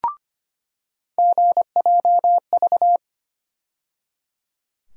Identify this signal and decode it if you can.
Morse code, heard as a keyed tone, about 25 wpm, 700 Hz. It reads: GJV
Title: USA Call Sign Postfixes - 3 Letters 25wpm